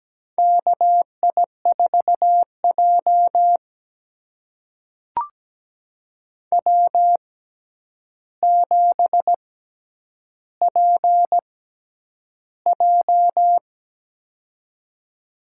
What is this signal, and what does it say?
Morse code, heard as a keyed tone, about 17 wpm, spaced as for 9 wpm, 700 Hz. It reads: KI4J W7PJ
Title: USA Call Signs - 4 Characters - 6x Character Spacing - 17wpm